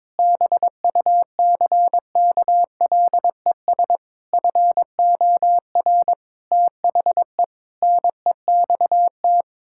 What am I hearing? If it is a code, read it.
Morse code, heard as a keyed tone, about 22 wpm, 700 Hz. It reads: BUCKLES FOR THE NEXT
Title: Pangram Challenge v2 - 22wpm